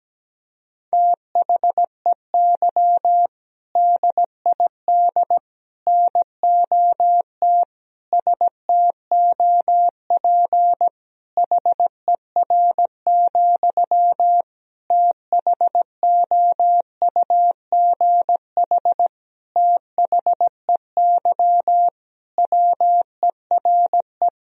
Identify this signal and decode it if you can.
Morse code, heard as a keyed tone, about 17 wpm, 700 Hz. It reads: THEY DID NOT STOP HER, THOUGH THEY WERE
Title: Sentences from Top 1000 Words - Review Part 2 - 17wpm